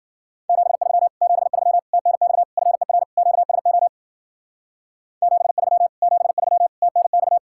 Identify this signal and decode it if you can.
Morse code, heard as a keyed tone, about 49 wpm, 700 Hz. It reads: 73 73 TNX FER QSO 73 73 TNX
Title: Speed-Racing Common QSO Phrases Encore - 40wpm